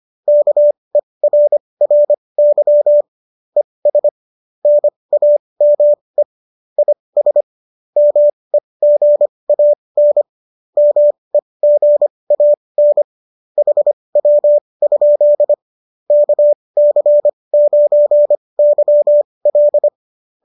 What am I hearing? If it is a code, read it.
Morse code, heard as a keyed tone, about 25 wpm, spaced as for 20 wpm, 590 Hz. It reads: KERRY ES NAME IS MEGAN MEGAN HW? KC9YL